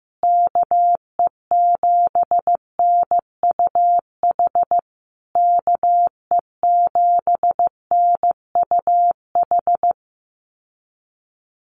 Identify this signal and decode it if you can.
Morse code, heard as a keyed tone, about 15 wpm, 700 Hz. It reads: KE7NUH KE7NUH
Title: USA Call Signs Repeated 3x 15wpm